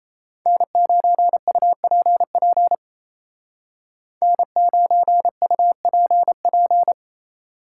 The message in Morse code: N9UPP N9UPP

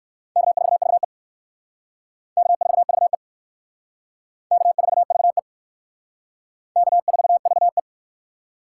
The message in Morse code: K4VE K4VE K4VE K4VE

